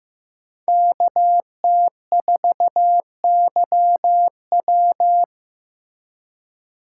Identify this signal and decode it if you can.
Morse code, heard as a keyed tone, about 18 wpm, 700 Hz. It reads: KT4YW
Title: Speed Racing - USA 2x2 Call Signs 15wpm